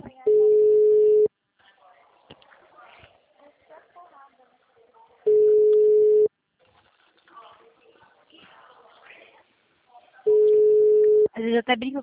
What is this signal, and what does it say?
Ringback tone (Europe, 425 Hz), with speech.
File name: out- -2012- - - .32899